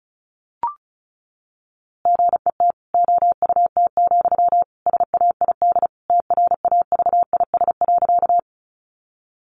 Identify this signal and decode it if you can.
Morse code, heard as a keyed tone, about 34 wpm, 700 Hz. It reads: GET OUT, SAID TRAVIS.
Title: A Sound Of Thunder 35wpm